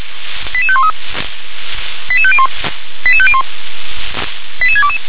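Five-tone selective call (ZVEI), heard as quick runs of five tones, tones of about 70 ms. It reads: 88513 88501 88501 88622
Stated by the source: ZVEI five-tone call 88731 received by SDR.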